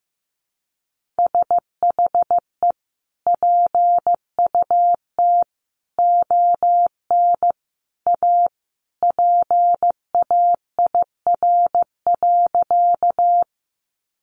Morse code, heard as a keyed tone, about 15 wpm, 700 Hz. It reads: SHE PUT ON A PAIR.
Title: Sentences from Top 1200 Words 15wpm